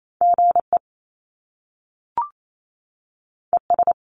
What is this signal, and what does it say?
Morse code, heard as a keyed tone, about 28 wpm, 700 Hz. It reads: GE ES
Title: Top Two Letter Combinations in Words 28wpm